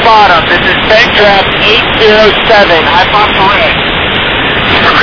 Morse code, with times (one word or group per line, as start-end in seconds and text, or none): none